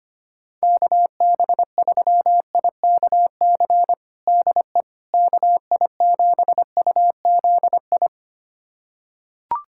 Morse code, heard as a keyed tone, about 25 wpm, 700 Hz. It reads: KB3IKC DE KI7UZI
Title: USA Callsign DE Callsign - 6 Characters - 25wpm